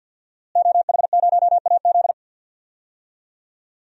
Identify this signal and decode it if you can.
Morse code, heard as a keyed tone, about 52 wpm, 700 Hz. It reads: OH0AZ
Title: DXCC-100 Call Signs - 5 Characters - Repeated 3x - 50wpm